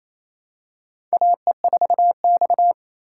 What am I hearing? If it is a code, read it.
Morse code, heard as a keyed tone, about 28 wpm, 700 Hz. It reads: AE4X
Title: USA Call Signs - 4 Characters - 3x Character Spacing - 28wpm